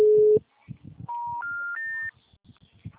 Special information tone.